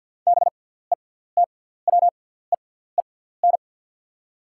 Morse code, heard as a keyed tone, about 48 wpm, spaced as for 17 wpm, 690 Hz. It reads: BETWEEN
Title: 6x Character Spacing - Top 500 Words - 50wpm